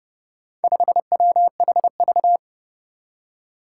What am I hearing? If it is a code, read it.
Morse code, heard as a keyed tone, about 30 wpm, 700 Hz. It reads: 5WHV
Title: Four Challenging Letter-Number Combinations 30wpm